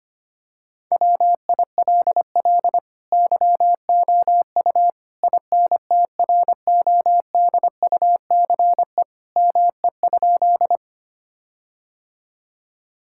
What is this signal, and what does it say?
Morse code, heard as a keyed tone, about 25 wpm, 700 Hz. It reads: WILL YOU INTRODUCE ME?
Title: Sentences from Top 1600 Words 25wpm